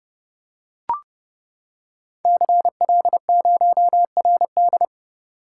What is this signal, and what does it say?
Morse code, heard as a keyed tone, about 30 wpm, 700 Hz. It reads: CL0RD
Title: Code Groups - 5 Letter:Numbers - 30wpm